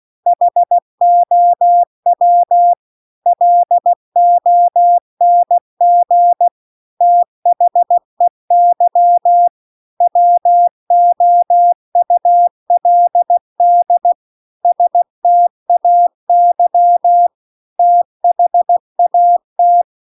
Morse code, heard as a keyed tone, about 16 wpm, 700 Hz. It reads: HOW LONG THEY WOULD STAY THAT